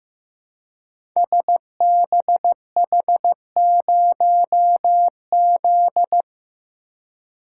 Morse code, heard as a keyed tone, about 15 wpm, 700 Hz. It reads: SBH0Z
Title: Code Groups - 5 Letter:Numbers - 15wpm